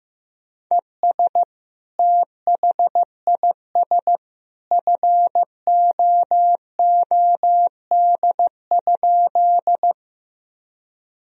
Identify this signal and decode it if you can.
Morse code, heard as a keyed tone, about 15 wpm, 700 Hz. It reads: ES THIS FOOD?
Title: Sentences from Top 300 Words and Abbreviations 15wpm